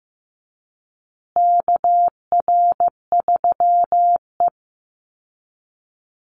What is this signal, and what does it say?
Morse code, heard as a keyed tone, about 14 wpm, 700 Hz. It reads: KR3E
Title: Call Signs Repeated 2x - 15wpm